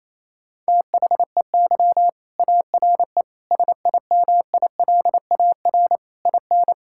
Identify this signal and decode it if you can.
Morse code, heard as a keyed tone, about 28 wpm, 700 Hz. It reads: THEY ARE SIMILAR IN
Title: Sentences from Top 1000 Words - Review Part 5 - 28wpm